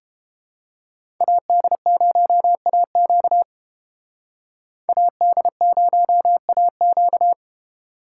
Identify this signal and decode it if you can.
Morse code, heard as a keyed tone, about 29 wpm, 700 Hz. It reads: AD0AQ AD0AQ